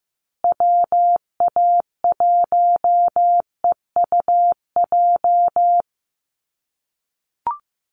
Morse code, heard as a keyed tone, about 15 wpm, 700 Hz. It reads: WA1EUJ
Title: USA Call Signs Repeated 3x 15wpm